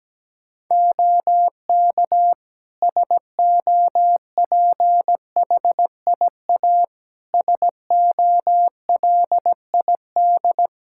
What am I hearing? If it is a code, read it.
Morse code, heard as a keyed tone, about 17 wpm, 700 Hz. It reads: OK SOPHIA SOLID
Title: Practice QSO - Exchange 3 - More Info - 17wpm